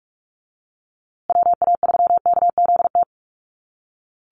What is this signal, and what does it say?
Morse code, heard as a keyed tone, about 46 wpm, 700 Hz. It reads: WA3KZT
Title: USA Call Signs Repeated 3x 45wpm